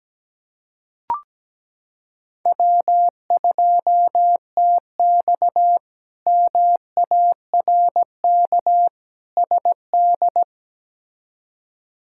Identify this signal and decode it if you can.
Morse code, heard as a keyed tone, about 17 wpm, 700 Hz. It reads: W2TX MARK SD
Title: K1USN SST QSO Practice - 17wpm